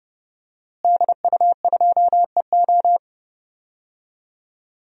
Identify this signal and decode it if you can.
Morse code, heard as a keyed tone, about 30 wpm, 700 Hz. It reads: DU2EO
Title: DXCC-100 Call Signs - 5 Characters - Repeated 3x - 30wpm